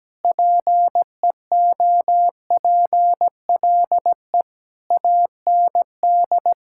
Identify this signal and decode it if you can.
Morse code, heard as a keyed tone, about 17 wpm, 700 Hz. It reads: PEOPLE AND
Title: Sentences from Top 1000 Words - Review Part 5 - 17wpm